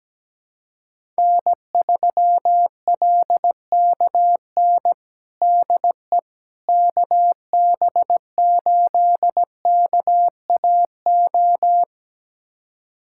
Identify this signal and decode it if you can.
Morse code, heard as a keyed tone, about 17 wpm, 700 Hz. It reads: N3LKN DE KB8KAO
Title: USA Callsign DE Callsign 17wpm